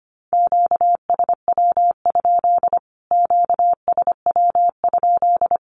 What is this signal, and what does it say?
Morse code, heard as a keyed tone, about 25 wpm, 700 Hz. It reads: QSW? QSW?
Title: Q-Codes Repeated 3x 25wpm